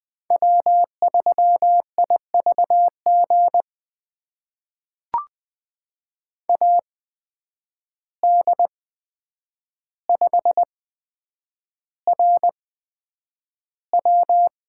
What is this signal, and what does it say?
Morse code, heard as a keyed tone, about 20 wpm, spaced as for 9 wpm, 700 Hz. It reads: W3IVG AD5RW
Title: USA Call Signs - 5 Characters - 8x Character Spacing - 20wpm